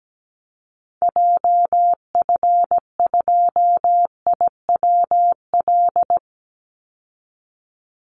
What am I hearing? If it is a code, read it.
Morse code, heard as a keyed tone, about 17 wpm, 700 Hz. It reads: JF2IWL
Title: Call Signs Repeated 2x - 17wpm